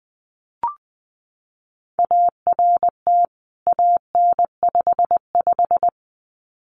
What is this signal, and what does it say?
Morse code, heard as a keyed tone, about 20 wpm, 700 Hz. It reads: ART AN55
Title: CWT Exchange 20wpm